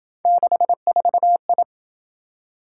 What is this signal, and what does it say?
Morse code, heard as a keyed tone, about 26 wpm, 700 Hz. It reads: 64I